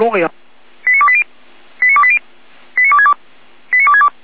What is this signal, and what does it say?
Five-tone selective call (ZVEI), heard as quick runs of five tones, tones of about 70 ms, with speech. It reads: 89280 89280 89262 89262
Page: SDR/ZVEI